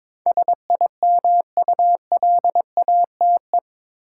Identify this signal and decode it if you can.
Morse code, heard as a keyed tone, about 21 wpm, 700 Hz. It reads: SIMULATE
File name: practice / 22wpm / Eight Letter Words - 22wpm